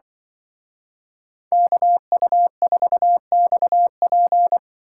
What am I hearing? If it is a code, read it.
Morse code, heard as a keyed tone, about 23 wpm, 700 Hz. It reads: KU4XP
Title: Speed Racing - USA Call Signs 20wpm